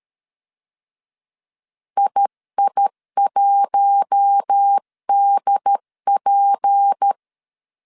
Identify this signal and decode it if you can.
Morse code, heard as a keyed tone, about 13 wpm, 800 Hz. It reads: II1DP
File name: CW / DET / 1 6 1500C S C 12wpm 3533